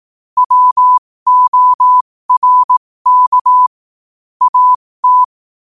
Morse code, heard as a keyed tone, about 18 wpm, spaced as for 15 wpm, 1000 Hz. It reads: WORK AT
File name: qotdfiles / 15wpm